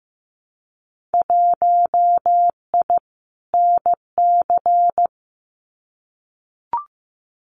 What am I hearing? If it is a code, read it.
Morse code, heard as a keyed tone, about 15 wpm, 700 Hz. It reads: 1I NC
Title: Winter Field Day Exchanges Encore 15wpm